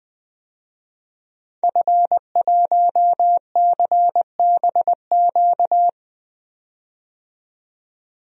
Morse code, heard as a keyed tone, about 20 wpm, 700 Hz. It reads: F1CBQ
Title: DXCC-100 Call Signs - 5 Characters - 20wpm